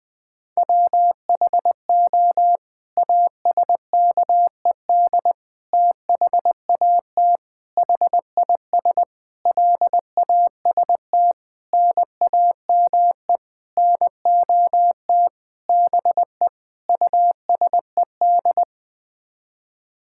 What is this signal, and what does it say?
Morse code, heard as a keyed tone, about 20 wpm, 700 Hz. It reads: WHO ASKED THAT HIS LAST NAME NOT BE USED
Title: Sets of 9 Words - 20wpm